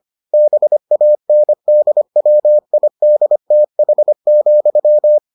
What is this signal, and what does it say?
Morse code, heard as a keyed tone, about 25 wpm, 600 Hz. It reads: BANDWIDTH,